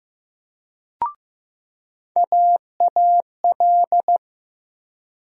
Morse code, heard as a keyed tone, about 15 wpm, 700 Hz. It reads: AAL
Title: USA Call Sign Postfixes - 3 Letters 15wpm